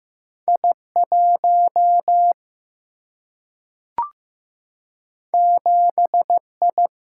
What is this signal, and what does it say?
Morse code, heard as a keyed tone, about 15 wpm, 700 Hz. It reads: I1 7I
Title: Single-Letter Single-Number 15wpm